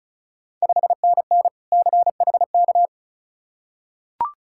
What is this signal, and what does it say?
Morse code, heard as a keyed tone, about 36 wpm, 700 Hz. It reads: 5NN CHK
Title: Ontario QSO Party Exchange 35wpm